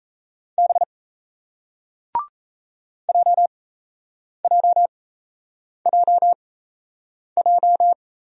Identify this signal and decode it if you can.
Morse code, heard as a keyed tone, about 38 wpm, spaced as for 8 wpm, 700 Hz. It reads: B JJJJ